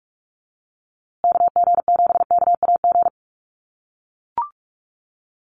Morse code, heard as a keyed tone, about 44 wpm, 700 Hz. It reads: KG7KAG
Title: USA Call Signs Repeated 3x 45wpm